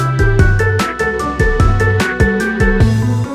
There is no code.